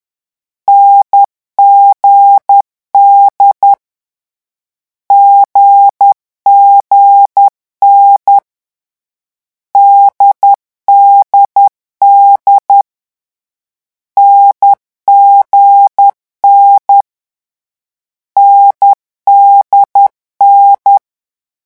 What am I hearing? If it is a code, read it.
Morse code, heard as a keyed tone, about 10 wpm, 800 Hz. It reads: NGD GGN DDD NGN NDN
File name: morse mp3 / lecon4-vitesse 10